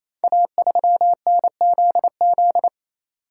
Morse code, heard as a keyed tone, about 29 wpm, 700 Hz. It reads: A3NZZ
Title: Code Groups - 5 Letter:Numbers - 28wpm